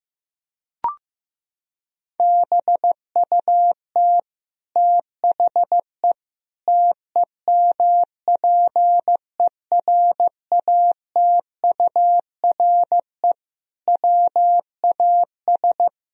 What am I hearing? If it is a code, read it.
Morse code, heard as a keyed tone, about 15 wpm, 700 Hz. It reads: BUT THE TEMPERATURE WAS
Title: Sentences from Top 1000 Words - Review Part 4 - 15wpm